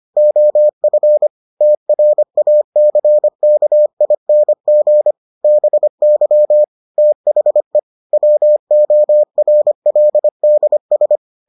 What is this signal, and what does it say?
Morse code, heard as a keyed tone, about 25 wpm, 600 Hz. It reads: OF TRACKING BY THE WORLDS